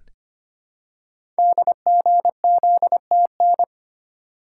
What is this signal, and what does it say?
Morse code, heard as a keyed tone, about 25 wpm, 700 Hz. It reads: DGZTN
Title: Code Groups - 5 Letter:Numbers - 25wpm